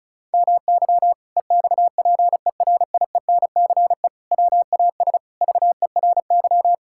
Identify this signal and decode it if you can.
Morse code, heard as a keyed tone, about 36 wpm, 700 Hz. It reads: MY EXPERIENCE WAS VERY
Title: Sentences from Top 1000 Words - Review Part 6 - 35wpm